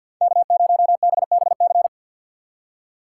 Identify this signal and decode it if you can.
Morse code, heard as a keyed tone, about 52 wpm, 700 Hz. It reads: K0BBC
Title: USA Call Signs - 3x Character Spacing - 50wpm